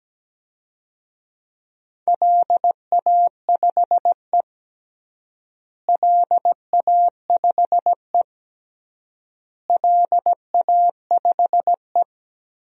Morse code, heard as a keyed tone, about 17 wpm, 700 Hz. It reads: LA5E LA5E LA5E
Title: DXCC-100 Call Signs - 4 Characters - Repeated 3x - 17wpm